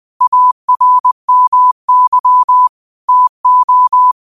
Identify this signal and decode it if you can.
Morse code, heard as a keyed tone, about 20 wpm, 1000 Hz. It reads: ARMY TO
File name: qotdfiles / 20wpm